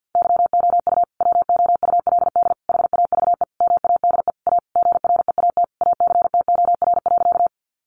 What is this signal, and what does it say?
Morse code, heard as a keyed tone, about 50 wpm, 700 Hz. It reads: YOU WOULD HAVE MADE A GREAT ACTOR.
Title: Sentences from Top 1500 Words 50wpm